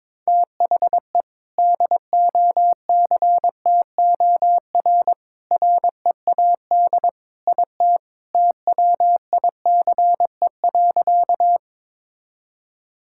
Morse code, heard as a keyed tone, about 22 wpm, 700 Hz. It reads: THE DOCTOR READ IT TWICE.
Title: Sentences from Top 1600 Words 22wpm